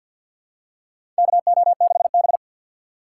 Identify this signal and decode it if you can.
Morse code, heard as a keyed tone, about 52 wpm, 700 Hz. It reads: KO6B
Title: USA Call Signs - 3x Character Spacing - 50wpm